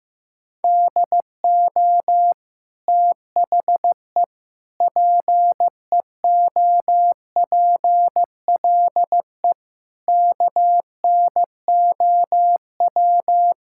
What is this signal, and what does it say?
Morse code, heard as a keyed tone, about 15 wpm, 700 Hz. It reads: DO THE PEOPLE KNOW
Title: Sentences from Top 1000 Words - Review Part 5 - 15wpm